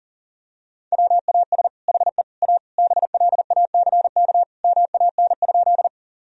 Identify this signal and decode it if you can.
Morse code, heard as a keyed tone, about 37 wpm, 700 Hz. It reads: WAS HE A BLACK MAN?
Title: Sentences from Top 1000 Words - Review Part 2 - 40wpm